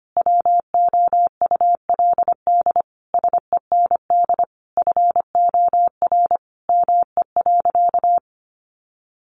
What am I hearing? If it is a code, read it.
Morse code, heard as a keyed tone, about 25 wpm, 700 Hz. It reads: WOULD SEND FOR ME.
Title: Sentences from Top 1500 Words 25wpm